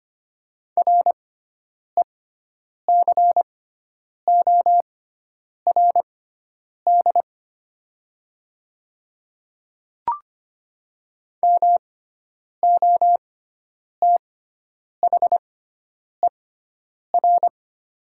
Morse code, heard as a keyed tone, about 25 wpm, spaced as for 9 wpm, 700 Hz. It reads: RECORD MOTHER
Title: 6x Character Spacing - Top 500 Words - 25wpm